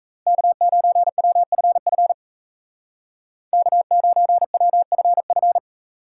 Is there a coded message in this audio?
Yes: Morse code from 0.3 to 5.6 s, heard as a keyed tone, about 36 wpm, 700 Hz. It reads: K9WFF K9WFF